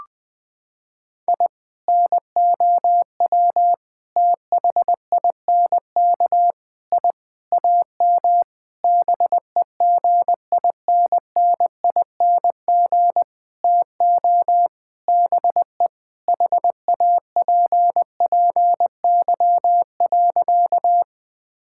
Morse code, heard as a keyed tone, about 20 wpm, 700 Hz. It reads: I NOW THINK I AM BEGINNING TO BE HAPPY.
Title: Sentences from Top 1000 Words - Review Part 5 - 20wpm